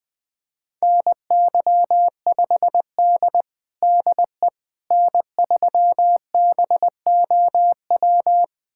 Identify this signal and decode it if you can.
Morse code, heard as a keyed tone, about 20 wpm, 700 Hz. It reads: NY5D DE N3BOW
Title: USA Callsign DE Callsign 20wpm